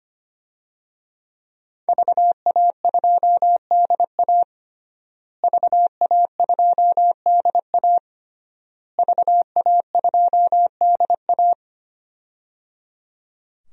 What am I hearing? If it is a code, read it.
Morse code, heard as a keyed tone, about 25 wpm, 700 Hz. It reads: VA2DA VA2DA VA2DA
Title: DXCC-100 Call Signs - 5 Characters - Repeated 3x - 25wpm